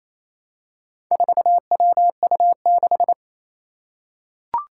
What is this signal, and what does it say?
Morse code, heard as a keyed tone, about 28 wpm, 700 Hz. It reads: 4WU6
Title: Four Challenging Letter-Number Combinations 28wpm